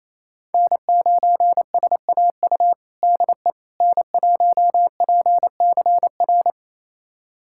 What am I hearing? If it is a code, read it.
Morse code, heard as a keyed tone, about 28 wpm, 700 Hz. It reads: N9SAU DE N1PCR
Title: USA Callsign DE Callsign - 5 Characters - 28wpm